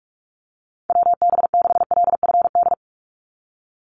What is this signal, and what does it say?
Morse code, heard as a keyed tone, about 44 wpm, 700 Hz. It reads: WB6LFD
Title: USA Call Signs 45wpm